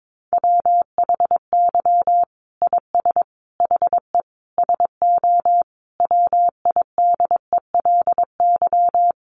Morse code, heard as a keyed tone, about 22 wpm, 700 Hz. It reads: WHY IS HE SO WIDELY
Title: Sentences from Top 1200 Words 22wpm